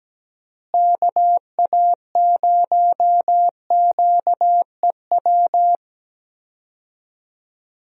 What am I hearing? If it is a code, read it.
Morse code, heard as a keyed tone, about 17 wpm, 700 Hz. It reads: KA0QEW
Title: USA Call Signs - 6 Characters - Repeated 2x - 17wpm